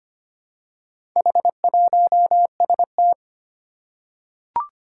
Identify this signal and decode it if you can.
Morse code, heard as a keyed tone, about 25 wpm, 700 Hz. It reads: H1ST
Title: Four Challenging Letter-Number Combinations 25wpm